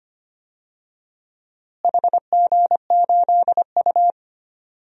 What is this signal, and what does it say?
Morse code, heard as a keyed tone, about 25 wpm, 700 Hz. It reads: HG8U
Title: DXCC-100 Call Signs - 4 Characters - Repeated 3x - 25wpm